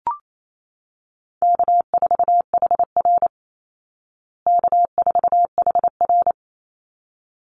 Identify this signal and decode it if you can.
Morse code, heard as a keyed tone, about 28 wpm, 700 Hz. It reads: K4HR K4HR
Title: Easy Call Signs Repeated 2x - 28wpm